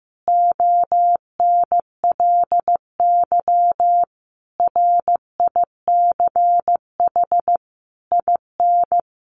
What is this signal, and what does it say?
Morse code, heard as a keyed tone, about 15 wpm, 700 Hz. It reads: ONLY RICH IN
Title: Sentences from Top 1100 Words 15wpm